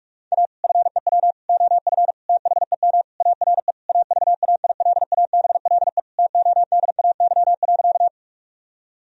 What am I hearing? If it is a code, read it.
Morse code, heard as a keyed tone, about 46 wpm, 700 Hz. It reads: A FEW OF THEM ARE AVAILABLE TODAY.
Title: Sentences from Top 1000 Words - Review Part 3 - 45wpm